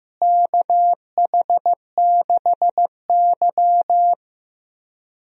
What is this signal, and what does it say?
Morse code, heard as a keyed tone, about 15 wpm, 700 Hz. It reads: KH6Y
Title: DXCC-100 Call Signs - 4 Characters - Repeated 2x - 15wpm